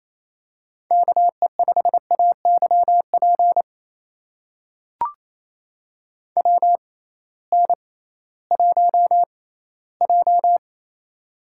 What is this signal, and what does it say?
Morse code, heard as a keyed tone, about 28 wpm, spaced as for 16 wpm, 700 Hz. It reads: KE5AYP WN1J
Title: USA Call Signs - 6x Character Spacing - 28wpm